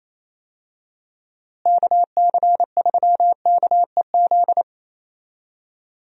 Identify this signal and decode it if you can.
Morse code, heard as a keyed tone, about 28 wpm, 700 Hz. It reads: KC3KEZ
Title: USA Call Signs - 6 Characters - Repeated 3x - 28wpm